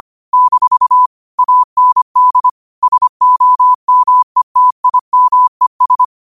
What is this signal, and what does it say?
Morse code, heard as a keyed tone, about 25 wpm, 1000 Hz. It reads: - AND SOMETIMES
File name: qotdfiles / 25wpm